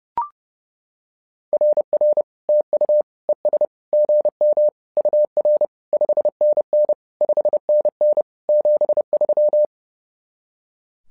Morse code, heard as a keyed tone, about 30 wpm, 600 Hz. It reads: RR TU ES GM UR 5NN 5NN 73
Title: SOTA QSO Practice 30wpm